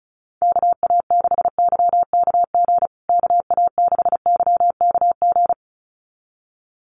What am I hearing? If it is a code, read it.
Morse code, heard as a keyed tone, about 34 wpm, 700 Hz. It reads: KA6YKG KA6YKG
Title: USA Call Signs Repeated 2x 35wpm